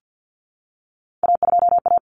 Morse code, heard as a keyed tone, about 50 wpm, 700 Hz. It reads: A2A
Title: Common Cut Numbers - 3-Digit 50wpm